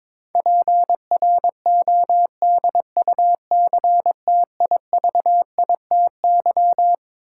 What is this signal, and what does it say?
Morse code, heard as a keyed tone, about 22 wpm, 700 Hz. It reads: PRODUCTIVITY
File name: Twelve Letter Words - 22wpm